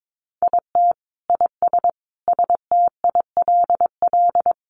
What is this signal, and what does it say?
Morse code, heard as a keyed tone, about 22 wpm, 700 Hz. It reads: IT IS STILL
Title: Easy Phrases 22wpm